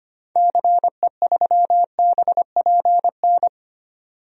Morse code, heard as a keyed tone, about 25 wpm, 700 Hz. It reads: CE3BPN
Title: DXCC-100 Call Signs - 6 Characters - 25wpm